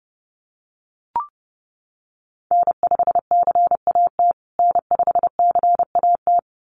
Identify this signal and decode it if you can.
Morse code, heard as a keyed tone, about 30 wpm, 700 Hz. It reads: N5CAT N5CAT
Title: USA Call Signs Repeated 2x 30wpm